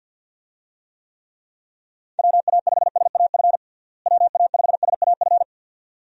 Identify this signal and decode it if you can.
Morse code, heard as a keyed tone, about 52 wpm, 700 Hz. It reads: WA5SAF WA5SAF
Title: USA Call Signs - 6 Characters - Repeated 2x - 50wpm